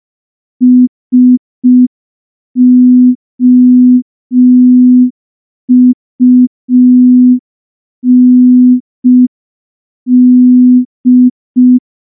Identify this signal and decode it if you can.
Morse code, heard as a keyed tone, about 5 wpm, 250 Hz. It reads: SOUND
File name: morse / morse-training-05WPM